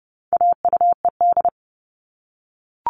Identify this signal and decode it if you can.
Morse code, heard as a keyed tone, about 30 wpm, 700 Hz. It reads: AUED
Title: All Cut Numbers - 4-Digit 30wpm